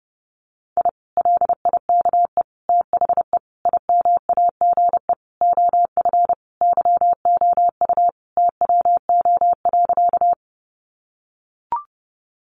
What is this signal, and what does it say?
Morse code, heard as a keyed tone, about 30 wpm, 700 Hz. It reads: I LIKE THE IMAGE OF YOU TWO.
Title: Sentences from Top 700 Words 30wpm